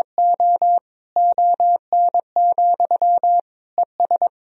Morse code, heard as a keyed tone, about 22 wpm, 700 Hz. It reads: O ON, ES
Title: Sentences from Top 100 Words and Abbreviations 22wpm